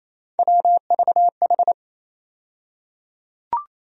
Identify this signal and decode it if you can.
Morse code, heard as a keyed tone, about 28 wpm, 700 Hz. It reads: WVH
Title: USA Call Sign Postfixes - 3 Letters 28wpm